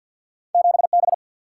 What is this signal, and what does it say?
Morse code, heard as a keyed tone, about 50 wpm, 700 Hz. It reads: 7 B
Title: Single-Letter Single-Number 50wpm